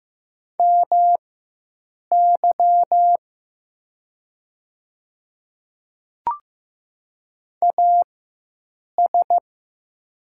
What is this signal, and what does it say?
Morse code, heard as a keyed tone, about 15 wpm, spaced as for 7 wpm, 700 Hz. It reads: MY AS
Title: 4x Character Spacing - Top 100 Words - 15wpm